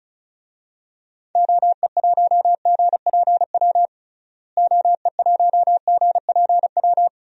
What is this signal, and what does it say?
Morse code, heard as a keyed tone, about 36 wpm, 700 Hz. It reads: OE1GPW OE1GPW
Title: DXCC-100 Call Signs - 6 Characters - Repeated 3x - 35wpm